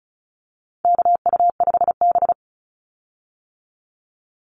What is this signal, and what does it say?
Morse code, heard as a keyed tone, about 34 wpm, 700 Hz. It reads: KU5B
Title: Call Signs - 35wpm